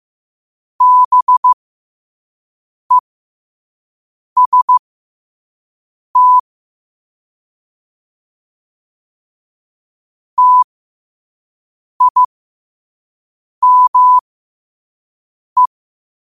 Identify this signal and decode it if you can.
Morse code, heard as a keyed tone, about 15 wpm, spaced as for 5 wpm, 1000 Hz. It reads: BEST TIME
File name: qotdfiles / 5wpm